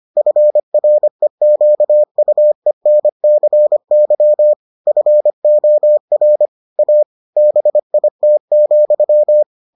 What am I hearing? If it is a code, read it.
Morse code, heard as a keyed tone, about 25 wpm, 600 Hz. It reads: FREQUENCY FOR A BIT,